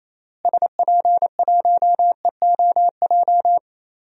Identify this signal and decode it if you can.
Morse code, heard as a keyed tone, about 28 wpm, 700 Hz. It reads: SP1EOJ
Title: DXCC-100 Call Signs - 6 Characters - Repeated 3x - 28wpm